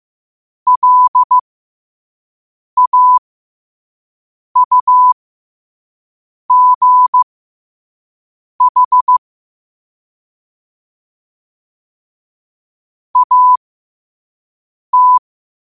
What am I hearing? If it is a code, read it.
Morse code, heard as a keyed tone, about 15 wpm, spaced as for 5 wpm, 1000 Hz. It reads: LAUGH AT
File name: qotdfiles / 5wpm